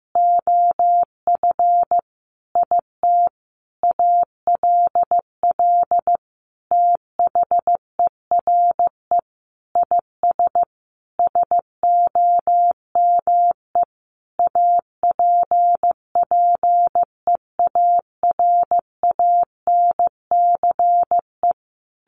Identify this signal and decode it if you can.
Morse code, heard as a keyed tone, about 14 wpm, 700 Hz. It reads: OF IT ALL THERE IS SOME APPEARANCE
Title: Sentences from Top 1100 Words 15wpm